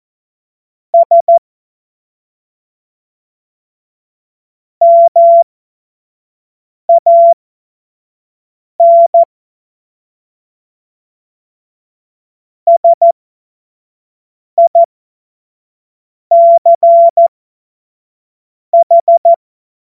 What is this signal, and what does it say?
Morse code, heard as a keyed tone, about 14 wpm, spaced as for 5 wpm, 680 Hz. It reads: S MAN SICH
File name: Morsekurs / Witze / Witz 06 Februar 2024 25 F70